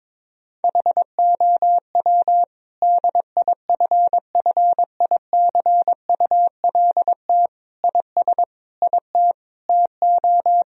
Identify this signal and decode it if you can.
Morse code, heard as a keyed tone, about 22 wpm, 700 Hz. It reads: HOW DIFFICULT IS IT TO
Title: Sentences from Top 1000 Words - Review Part 3 - 22wpm